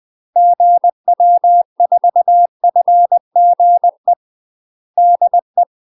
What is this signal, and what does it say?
Morse code, heard as a keyed tone, about 20 wpm, 700 Hz. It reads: GW4FGE DE
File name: UK MASTER QSOs TRAINING / 20WPM / UK MASTER QSOs 0023